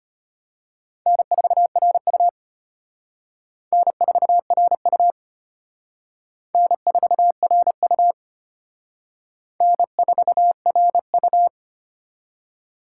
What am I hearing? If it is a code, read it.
Morse code, heard as a keyed tone, about 33 wpm, 700 Hz. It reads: N4RU N4RU N4RU N4RU